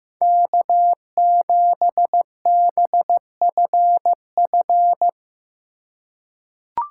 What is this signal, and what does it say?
Morse code, heard as a keyed tone, about 15 wpm, 700 Hz. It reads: K7BFF
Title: USA Call Signs - 5 Characters - 3x Character Spacing - 15wpm